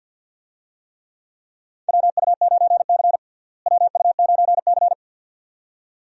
Morse code, heard as a keyed tone, about 47 wpm, 700 Hz. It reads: WU9C WU9C
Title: USA Call Signs - 4 Characters - Repeated 2x - 50wpm